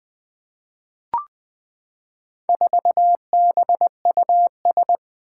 Morse code, heard as a keyed tone, about 20 wpm, 700 Hz. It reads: 4BUS
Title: Four Challenging Letter-Number Combinations 20wpm